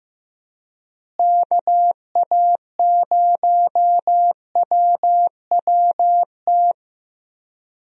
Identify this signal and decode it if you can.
Morse code, heard as a keyed tone, about 15 wpm, 700 Hz. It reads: KA0WWT
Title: USA Call Signs - 6 Characters - Repeated 3x - 15wpm